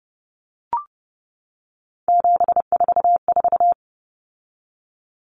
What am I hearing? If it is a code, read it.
Morse code, heard as a keyed tone, about 30 wpm, 700 Hz. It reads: 744
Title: Numbers - 3-Digit 30wpm